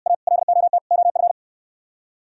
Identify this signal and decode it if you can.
Morse code, heard as a keyed tone, about 68 wpm, 700 Hz. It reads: A LOT OF